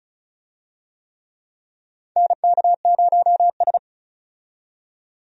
Morse code, heard as a keyed tone, about 36 wpm, 700 Hz. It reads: NK0S
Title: USA Call Signs - 4 Characters - 35wpm